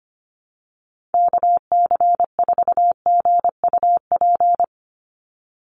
Morse code, heard as a keyed tone, about 25 wpm, 700 Hz. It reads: KC4GUP
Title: USA Call Signs 25wpm